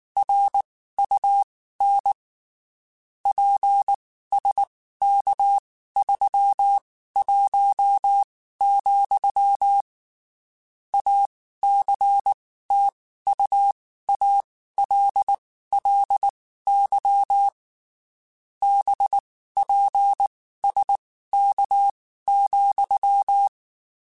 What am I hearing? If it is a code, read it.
Morse code, heard as a keyed tone, about 19 wpm, spaced as for 14 wpm, 800 Hz. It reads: RUN PSK31, ACTUALLY BPSK,
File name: (spaced) / QSOs Spaced-mp3s / WA7QOX 2 @18 sp